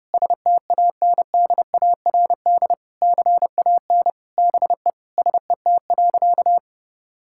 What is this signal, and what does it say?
Morse code, heard as a keyed tone, about 32 wpm, 700 Hz. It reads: STANDARD CAN BE SET.
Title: Sentences from Top 1000 Words - Review Part 6 - 30wpm